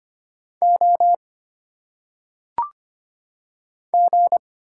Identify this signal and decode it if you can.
Morse code, heard as a keyed tone, about 25 wpm, 700 Hz. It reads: O G
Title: Single Letters - Phonetic Pronunciation 25wpm